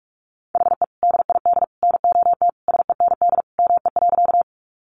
Morse code, heard as a keyed tone, about 44 wpm, 700 Hz. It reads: HE DID NOT SEND ME.
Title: Sentences from Top 1500 Words 45wpm